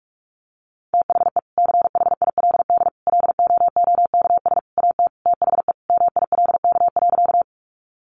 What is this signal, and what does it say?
Morse code, heard as a keyed tone, about 44 wpm, 700 Hz. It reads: THE CHILD LOOKS AT THE MILK.
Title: Additional Easy Phrases 45wpm